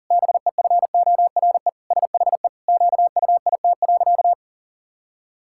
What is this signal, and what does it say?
Morse code, heard as a keyed tone, about 40 wpm, 700 Hz. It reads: BEFORE SHE QUIT.
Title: Pangram Challenge v2 - 40wpm